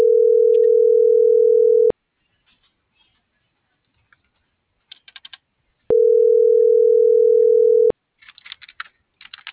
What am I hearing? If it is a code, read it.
Ringback tone (North America), 440 + 480 Hz.